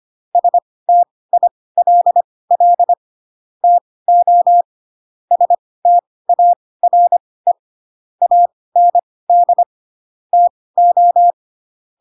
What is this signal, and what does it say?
Morse code, heard as a keyed tone, about 25 wpm, spaced as for 18 wpm, 700 Hz. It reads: STILL TO STARE AND TO
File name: Pinocchio ch3 12 25 ik1hge